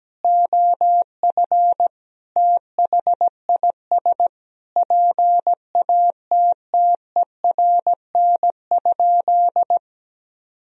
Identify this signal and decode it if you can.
Morse code, heard as a keyed tone, about 17 wpm, 700 Hz. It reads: OF THIS PATTERN?
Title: Sentences from Top 1000 Words - Review Part 2 - 17wpm